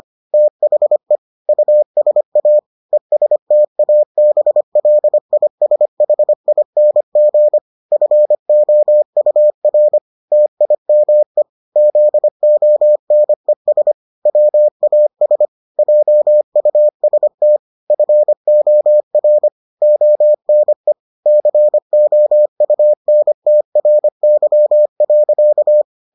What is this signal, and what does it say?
Morse code, heard as a keyed tone, about 25 wpm, 600 Hz. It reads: THE USA ESTABLISHING FOUR TIME ZONES WAS JUST FOR ONE COUNTRY.